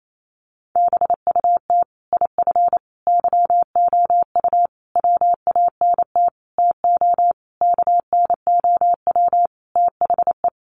Morse code, heard as a keyed tone, about 28 wpm, 700 Hz. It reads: BUT IF YOU WANT TO KNOW THE
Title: Sentences from Top 800 Words 28wpm